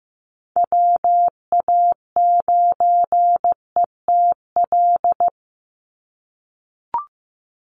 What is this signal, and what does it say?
Morse code, heard as a keyed tone, about 15 wpm, 700 Hz. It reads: WA9ETL
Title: USA Call Signs 15wpm